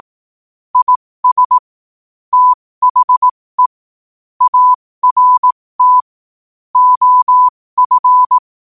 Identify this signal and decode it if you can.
Morse code, heard as a keyed tone, about 18 wpm, spaced as for 15 wpm, 1000 Hz. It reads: IS THE ART OF
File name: qotdfiles / 15wpm